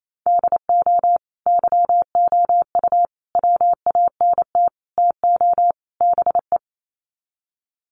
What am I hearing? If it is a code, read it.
Morse code, heard as a keyed tone, about 28 wpm, 700 Hz. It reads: DO YOU WANT TO BE
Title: Sets of 5 Words Encore2 28wpm